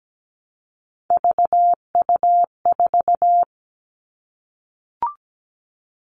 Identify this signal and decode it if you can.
Morse code, heard as a keyed tone, about 17 wpm, 700 Hz. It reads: VU4
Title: All Cut Numbers - 3-Digit 17wpm